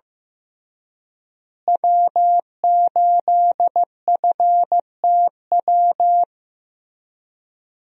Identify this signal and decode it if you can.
Morse code, heard as a keyed tone, about 15 wpm, 700 Hz. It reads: W8FTW
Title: USA Call Signs - 5 Characters - Repeated 2x - 15wpm